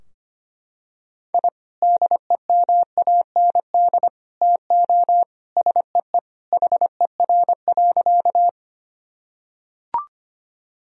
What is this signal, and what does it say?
Morse code, heard as a keyed tone, about 25 wpm, 700 Hz. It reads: I DEMAND TO SEE HER.
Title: Sentences from Top 1000 Words - Review Part 4 - 25wpm